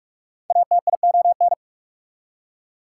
Morse code, heard as a keyed tone, about 44 wpm, 690 Hz. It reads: ATION
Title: Prefixes and Suffixes 45wpm